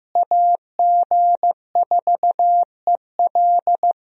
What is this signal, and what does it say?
Morse code, heard as a keyed tone, about 15 wpm, 700 Hz. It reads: AG4EL
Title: USA Call Signs - 5 Characters - Repeated 3x - 15wpm